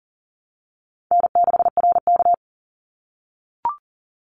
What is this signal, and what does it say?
Morse code, heard as a keyed tone, about 40 wpm, 700 Hz. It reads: N6RK
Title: Easy Call Signs - 40wpm